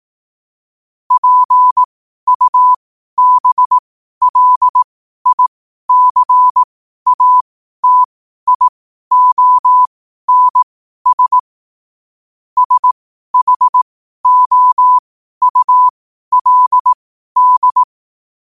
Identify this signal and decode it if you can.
Morse code, heard as a keyed tone, about 18 wpm, spaced as for 12 wpm, 1000 Hz. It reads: PUBLICATIONS SHOULD
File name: qotdfiles / 12wpm